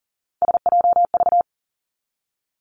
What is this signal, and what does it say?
Morse code, heard as a keyed tone, about 42 wpm, 700 Hz. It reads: SJV
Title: ARRL Sections 40wpm